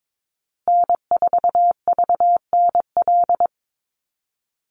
Morse code, heard as a keyed tone, about 22 wpm, 700 Hz. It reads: N4VNL
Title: USA Call Signs Repeated 3x 22wpm